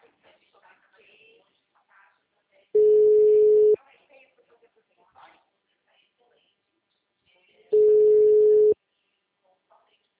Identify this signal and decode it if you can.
Ringback tone (Europe, 425 Hz), with speech.